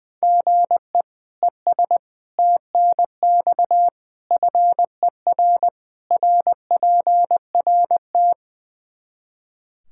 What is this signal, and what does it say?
Morse code, heard as a keyed tone, about 20 wpm, 700 Hz. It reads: GE ES TNX FER RPRT
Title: Practice QSO - Exchange 1 - Beginning - 20wpm